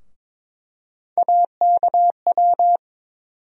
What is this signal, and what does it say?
Morse code, heard as a keyed tone, about 22 wpm, 700 Hz. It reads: AKW
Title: USA Call Sign Postfixes - 3 Letters 22wpm